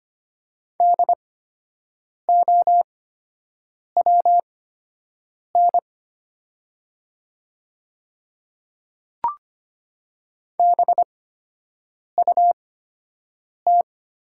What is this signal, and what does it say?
Morse code, heard as a keyed tone, about 25 wpm, spaced as for 7 wpm, 700 Hz. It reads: DOWN BUT
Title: 8x Character Spacing - Top 100 Words - 25wpm